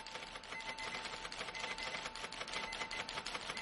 Emergency Alert System (SAME) tones, heard as harsh buzzing bursts. It is heard from 0.5 to 3.0 s.